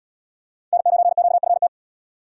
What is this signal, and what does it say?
Morse code, heard as a keyed tone, about 70 wpm, 700 Hz. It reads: N9QFT